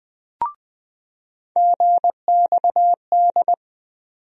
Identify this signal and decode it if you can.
Morse code, heard as a keyed tone, about 21 wpm, 700 Hz. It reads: GXD
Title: USA Call Sign Postfixes - 3 Letters 20wpm